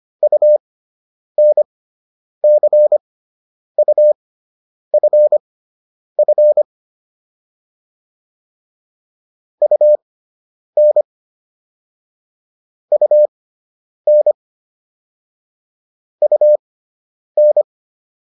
Morse code, heard as a keyed tone, about 25 wpm, spaced as for 9 wpm, 600 Hz. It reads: UNCUFF UN UN UN